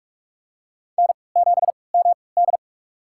Morse code, heard as a keyed tone, about 47 wpm, spaced as for 26 wpm, 700 Hz. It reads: N7MD
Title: USA Call Signs - 4 Characters - 3x Character Spacing - 45wpm